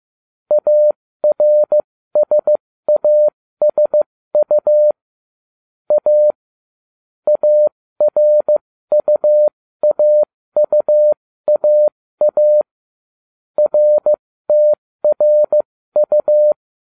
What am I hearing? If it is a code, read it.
Morse code, heard as a keyed tone, about 15 wpm, spaced as for 13 wpm, 600 Hz. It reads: ARSASU A ARUAUAA RTRU